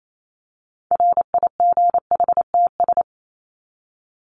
Morse code, heard as a keyed tone, about 28 wpm, 700 Hz. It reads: RIGHTS
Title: ES:S Suffix - Top 100 Words with Base Words 28wpm